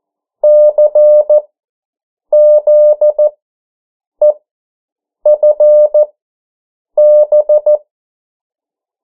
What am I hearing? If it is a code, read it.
Morse code, heard as a keyed tone, about 14 wpm, spaced as for 7 wpm, 600 Hz. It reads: CZEFB
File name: morse mp3 / 09 urok 8